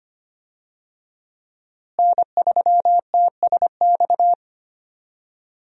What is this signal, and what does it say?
Morse code, heard as a keyed tone, about 25 wpm, 700 Hz. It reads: N3TSX
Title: USA Call Signs - 5 Characters - Repeated 2x - 25wpm